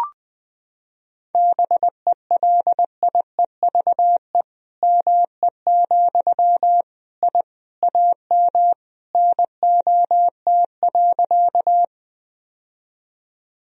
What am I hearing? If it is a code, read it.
Morse code, heard as a keyed tone, about 20 wpm, 700 Hz. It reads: BELIEVE ME, I AM NOT.
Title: Sentences from Top 1000 Words - Review Part 2 - 20wpm